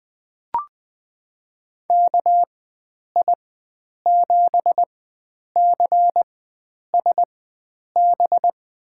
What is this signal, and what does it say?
Morse code, heard as a keyed tone, about 21 wpm, spaced as for 9 wpm, 700 Hz. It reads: KI7CSB
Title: USA Call Signs - 6 Characters - 4x Character Spacing - 20wpm